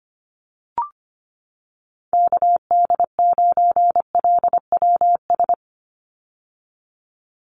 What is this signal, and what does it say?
Morse code, heard as a keyed tone, about 25 wpm, 700 Hz. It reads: KD9LWS
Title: USA Call Signs 25wpm